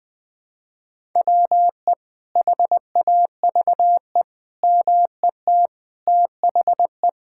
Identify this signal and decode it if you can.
Morse code, heard as a keyed tone, about 20 wpm, 700 Hz. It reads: WE HAVE MET THE
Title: Sets of 9 Words - 20wpm